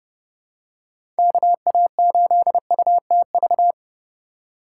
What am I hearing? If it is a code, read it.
Morse code, heard as a keyed tone, about 30 wpm, 700 Hz. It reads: KA8UTV
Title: USA Call Signs - 6 Characters - 3x Character Spacing - 30wpm